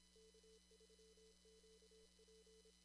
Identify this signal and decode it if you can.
Morse code, heard as a keyed tone, about 25 wpm, 440 Hz, with noise. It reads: K3QP
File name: CW / kt5j wpx cw-op-k5pi